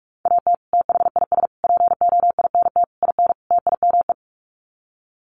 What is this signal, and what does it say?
Morse code, heard as a keyed tone, about 44 wpm, 700 Hz. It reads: AT THIS POINT IN TIME
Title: Sets of 5 Words Encore 45wpm